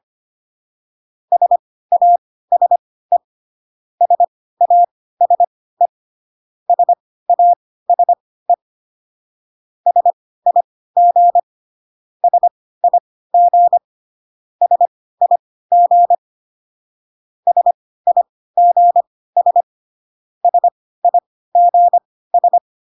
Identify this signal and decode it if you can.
Morse code, heard as a keyed tone, about 25 wpm, spaced as for 16 wpm, 700 Hz. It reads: SASE SASE SASE SIG SIG SIG SIGS SIGS